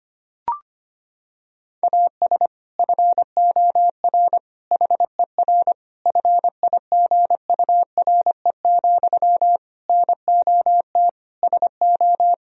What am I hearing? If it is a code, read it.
Morse code, heard as a keyed tone, about 25 wpm, 700 Hz. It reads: AS FOR HER FIGURE, NOT SO
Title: Sentences from Top 1000 Words - Review Part 2 - 25wpm